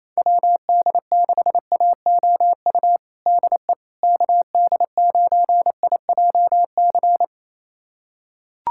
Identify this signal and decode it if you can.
Morse code, heard as a keyed tone, about 28 wpm, 700 Hz. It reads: WD6AOU DE KD9IJC
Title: USA Callsign DE Callsign - 6 Characters - 28wpm